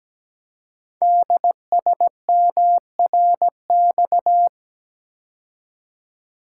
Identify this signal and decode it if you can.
Morse code, heard as a keyed tone, about 17 wpm, 700 Hz. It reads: DSMRX
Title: Code Groups - 5 Letters - 17wpm